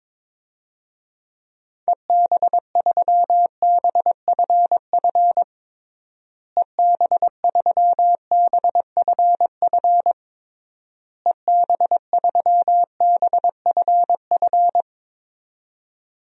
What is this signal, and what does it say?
Morse code, heard as a keyed tone, about 21 wpm, 700 Hz. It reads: EB3BFF EB3BFF EB3BFF
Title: DXCC-100 Call Signs - 6 Characters - Repeated 3x - 22wpm